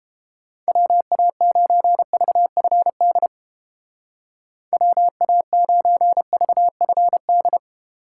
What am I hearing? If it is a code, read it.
Morse code, heard as a keyed tone, about 30 wpm, 700 Hz. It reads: WA9VFD WA9VFD